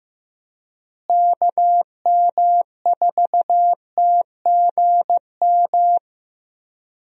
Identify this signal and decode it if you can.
Morse code, heard as a keyed tone, about 15 wpm, 700 Hz. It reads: KM4TGM
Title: USA Call Signs - 6 Characters - Repeated 2x - 15wpm